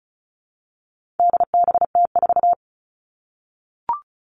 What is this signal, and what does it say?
Morse code, heard as a keyed tone, about 34 wpm, 700 Hz. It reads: DBT4
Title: All Cut Numbers - 4-Digit 35wpm